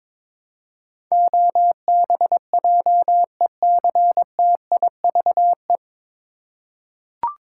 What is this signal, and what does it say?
Morse code, heard as a keyed tone, about 22 wpm, 700 Hz. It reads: OBJECTIVE
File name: 9 Letter Words - First 3 Letters Given - 22wpm